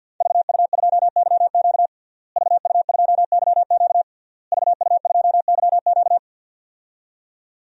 Morse code, heard as a keyed tone, about 51 wpm, 700 Hz. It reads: VU2YQ VU2YQ VU2YQ
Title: DXCC-100 Call Signs - 5 Characters - Repeated 3x - 50wpm